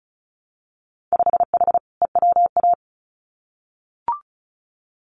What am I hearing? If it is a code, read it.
Morse code, heard as a keyed tone, about 36 wpm, 700 Hz. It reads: 5H EWA
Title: Winter Field Day Exchanges Encore 35wpm